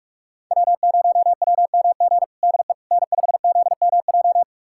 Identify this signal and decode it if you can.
Morse code, heard as a keyed tone, about 47 wpm, 700 Hz. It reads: W0WMG DE N5ZMJ
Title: USA Callsign DE Callsign - 5 Characters - 45wpm